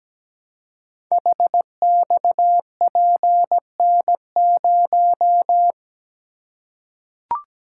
Morse code, heard as a keyed tone, about 17 wpm, 700 Hz. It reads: HXPN0
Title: Code Groups - 5 Letter:Numbers - 17wpm